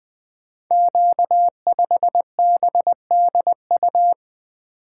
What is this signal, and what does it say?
Morse code, heard as a keyed tone, about 20 wpm, 700 Hz. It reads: Q5BDU
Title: Code Groups - 5 Letter:Numbers - 20wpm